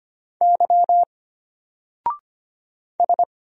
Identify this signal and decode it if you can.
Morse code, heard as a keyed tone, about 25 wpm, 700 Hz. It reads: Y S
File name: Single-Letter-Number-25wpm